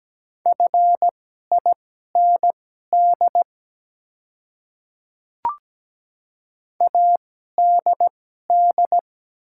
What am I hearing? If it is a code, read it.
Morse code, heard as a keyed tone, about 17 wpm, spaced as for 12 wpm, 700 Hz. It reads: FIND ADD
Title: 2x Character Spacing - Top 200 Words - 17wpm